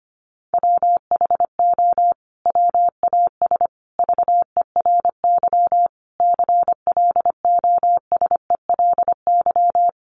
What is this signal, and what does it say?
Morse code, heard as a keyed tone, about 25 wpm, 700 Hz. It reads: WHO WAS VERY CLOSELY
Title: Sentences from Top 1400 Words 25wpm